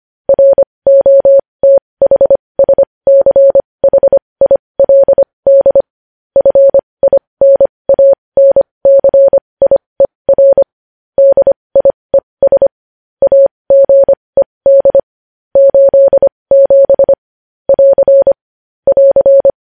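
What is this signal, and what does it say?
Morse code, heard as a keyed tone, about 25 wpm, spaced as for 20 wpm, 550 Hz. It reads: ROTHSCHILD FINANCIER DIES AGED 87 <AR> <AR>
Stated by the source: Morse code news headlines sent at 20 wpm.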